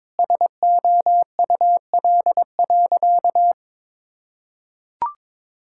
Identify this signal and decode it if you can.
Morse code, heard as a keyed tone, about 22 wpm, 700 Hz. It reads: SOUL.
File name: Sentences with 9-Word Phrases - 22wpm